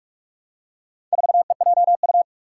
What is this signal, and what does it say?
Morse code, heard as a keyed tone, about 46 wpm, 700 Hz. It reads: 4EJU
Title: Four Challenging Letter-Number Combinations 45wpm